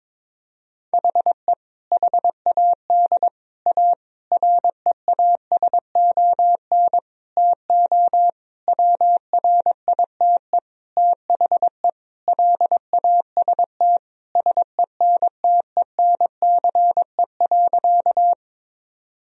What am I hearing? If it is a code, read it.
Morse code, heard as a keyed tone, about 22 wpm, 700 Hz. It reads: HE HAD A REASON TO WRITE THE LAST SENTENCE.
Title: Sentences from Top 1000 Words - Review Part 1 - 22wpm